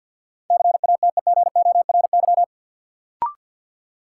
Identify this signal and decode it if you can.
Morse code, heard as a keyed tone, about 50 wpm, 700 Hz. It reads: CATEGORY
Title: Sentences from Top 1000 Words - Review Part 7 - New Word Drill - 50wpm